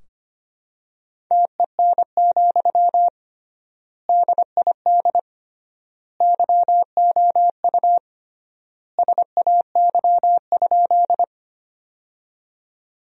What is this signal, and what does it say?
Morse code, heard as a keyed tone, about 25 wpm, 700 Hz. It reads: TEN, DID YOU SAY?
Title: 3x Word-Spacing - Sentences from Top 400 Words - 25wpm